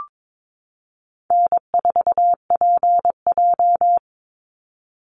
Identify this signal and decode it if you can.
Morse code, heard as a keyed tone, about 22 wpm, 700 Hz. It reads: N4PJ
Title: Easy Call Signs Repeated 2x - 22wpm